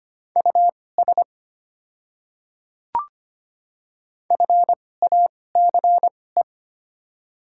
Morse code, heard as a keyed tone, about 25 wpm, spaced as for 18 wpm, 690 Hz. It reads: US FACE
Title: 2x Character Spacing - Top 300 Words - 25wpm